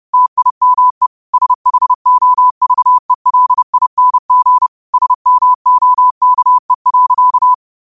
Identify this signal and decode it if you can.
Morse code, heard as a keyed tone, about 30 wpm, 1000 Hz. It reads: TIME SHOVELING SMOKE.